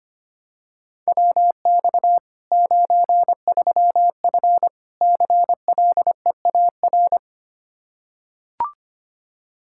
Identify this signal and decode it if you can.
Morse code, heard as a keyed tone, about 25 wpm, 700 Hz. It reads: WX 93F CLEAR
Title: Practice QSO - Exchange 2 - Info - 25wpm